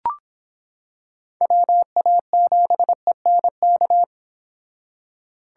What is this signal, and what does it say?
Morse code, heard as a keyed tone, about 25 wpm, 700 Hz. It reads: WA7ENK